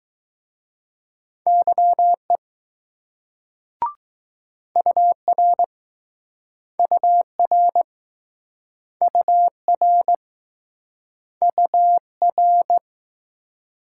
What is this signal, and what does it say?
Morse code, heard as a keyed tone, about 20 wpm, 700 Hz. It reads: YE UR UR UR UR